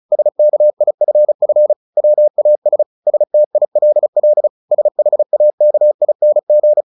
Morse code, heard as a keyed tone, about 35 wpm, 600 Hz. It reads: SKIFF WAS STILL SHAKING
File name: the old man and the sea 2 35wpm